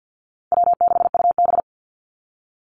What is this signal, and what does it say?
Morse code, heard as a keyed tone, about 50 wpm, 700 Hz. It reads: W6UB
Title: NA SOTA Activator Callsigns 50wpm